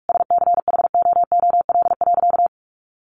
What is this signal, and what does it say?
Morse code, heard as a keyed tone, about 44 wpm, 700 Hz. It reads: SCHOOL.
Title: Sentences from Top 1100 Words 45wpm